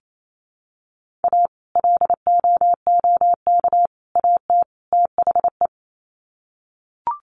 Morse code, heard as a keyed tone, about 28 wpm, 700 Hz. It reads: A LOOK AT THE
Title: Sets of 4 Words 28wpm